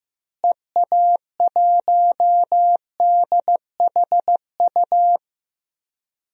Morse code, heard as a keyed tone, about 16 wpm, 700 Hz. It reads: EA1DHU
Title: DXCC-100 Call Signs - 6 Characters - Repeated 3x - 15wpm